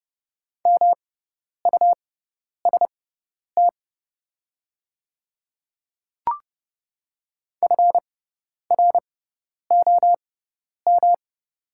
Morse code, heard as a keyed tone, about 30 wpm, spaced as for 10 wpm, 700 Hz. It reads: MUST FROM
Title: 6x Character Spacing - Top 200 Words - 30wpm